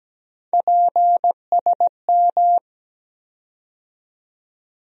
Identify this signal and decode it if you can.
Morse code, heard as a keyed tone, about 17 wpm, 700 Hz. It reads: PSM
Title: USA Call Sign Postfixes - 3 Letters 17wpm